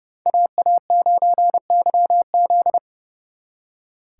Morse code, heard as a keyed tone, about 31 wpm, 700 Hz. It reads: AA9YZ
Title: Speed Racing - USA Call Signs 30wpm